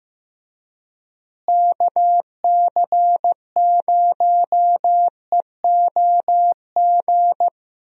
Morse code, heard as a keyed tone, about 15 wpm, 700 Hz. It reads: KC0EOG
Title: USA Call Signs - 6 Characters - Repeated 3x - 15wpm